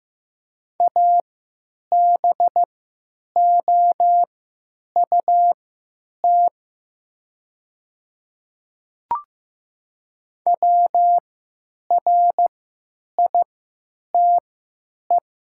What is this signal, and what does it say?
Morse code, heard as a keyed tone, about 15 wpm, spaced as for 9 wpm, 700 Hz. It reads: ABOUT WRITE
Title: 3x Character Spacing - Top 100 Words - 15wpm